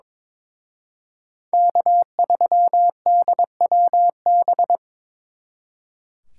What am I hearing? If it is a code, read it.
Morse code, heard as a keyed tone, about 26 wpm, 700 Hz. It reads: K3DWB
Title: Speed Racing - USA 1x3 Call Signs 22wpm